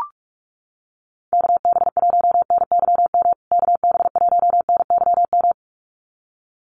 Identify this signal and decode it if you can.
Morse code, heard as a keyed tone, about 44 wpm, 700 Hz. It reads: KB1NYM KB1NYM
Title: USA Call Signs Repeated 2x 45wpm